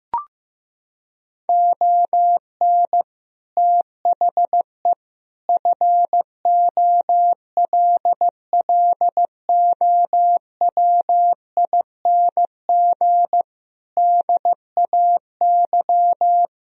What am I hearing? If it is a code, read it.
Morse code, heard as a keyed tone, about 15 wpm, 700 Hz. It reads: ON THE FOLLOWING DAY
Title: Sentences from Top 1000 Words - Review Part 4 - 15wpm